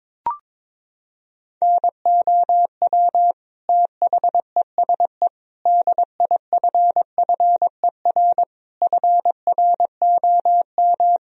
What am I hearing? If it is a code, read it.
Morse code, heard as a keyed tone, about 22 wpm, 700 Hz. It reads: NOW THESE DIFFER FROM
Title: Sentences from Top 1000 Words - Review Part 1 - 22wpm